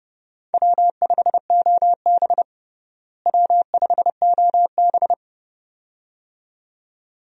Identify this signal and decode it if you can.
Morse code, heard as a keyed tone, about 29 wpm, 700 Hz. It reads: W5OB W5OB
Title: USA Call Signs - 4 Characters - Repeated 2x - 30wpm